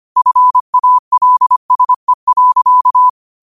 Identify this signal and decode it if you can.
Morse code, heard as a keyed tone, about 25 wpm, 1000 Hz. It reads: FALSE.
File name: qotdfiles / 25wpm